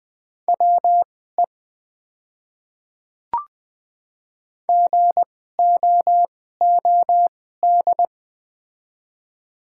Morse code, heard as a keyed tone, about 20 wpm, spaced as for 15 wpm, 690 Hz. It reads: WE GOOD
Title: 2x Character Spacing - Top 200 Words - 20wpm